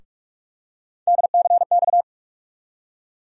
Morse code, heard as a keyed tone, about 46 wpm, 700 Hz. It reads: DCX
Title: USA Call Sign Postfixes - 3 Letters 45wpm